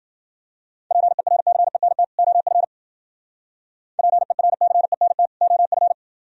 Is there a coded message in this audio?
Yes: Morse code from 0.9 to 5.9 s, heard as a keyed tone, about 51 wpm, 700 Hz. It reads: PERCENT OF PERCENT OF